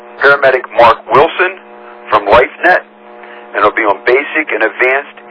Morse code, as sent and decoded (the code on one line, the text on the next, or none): none